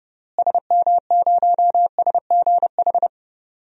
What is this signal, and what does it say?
Morse code, heard as a keyed tone, about 30 wpm, 700 Hz. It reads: SM0SGH
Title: DXCC-100 Call Signs - 6 Characters - Repeated 2x - 30wpm